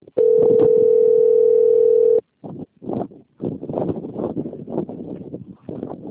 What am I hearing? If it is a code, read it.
Ringback tone (North America), 440 + 480 Hz.
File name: out- - - - - .10763